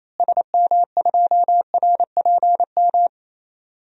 Morse code, heard as a keyed tone, about 28 wpm, 700 Hz. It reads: SM2RPM
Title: DXCC-100 Call Signs - 6 Characters - Repeated 3x - 28wpm